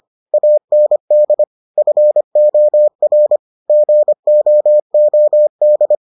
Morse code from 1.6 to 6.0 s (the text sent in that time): FOR GOOD